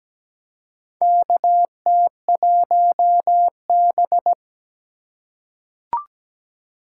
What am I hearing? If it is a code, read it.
Morse code, heard as a keyed tone, about 17 wpm, 700 Hz. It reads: KT1B
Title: K1USN SST QSO Practice - 17wpm